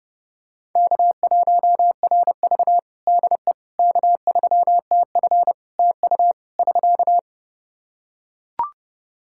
Morse code, heard as a keyed tone, about 29 wpm, 700 Hz. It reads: K1RV DE K3TF TU <SK>
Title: Practice QSO - Exchange 4 - Ending - 30wpm